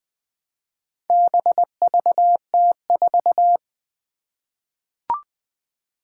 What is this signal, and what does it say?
Morse code, heard as a keyed tone, about 20 wpm, 700 Hz. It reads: BVT4
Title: Four Challenging Letter-Number Combinations 20wpm